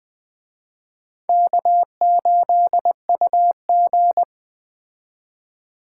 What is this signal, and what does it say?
Morse code, heard as a keyed tone, about 21 wpm, 700 Hz. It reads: K8UG
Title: Speed Racing - USA 1x2 Call Signs 17wpm